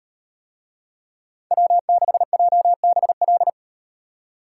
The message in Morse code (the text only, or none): W6JBL